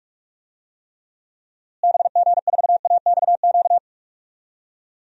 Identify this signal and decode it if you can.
Morse code, heard as a keyed tone, about 46 wpm, 700 Hz. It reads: BG4AXQ
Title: DXCC-100 Call Signs - 6 Characters - Repeated 2x - 45wpm